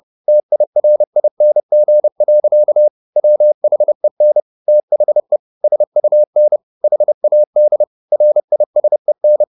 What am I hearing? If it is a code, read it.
Morse code, heard as a keyed tone, about 30 wpm, 600 Hz. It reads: TIRING. WHEN THE SUN HAD RISEN
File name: the old man and the sea / the old man and the sea 1 30wpm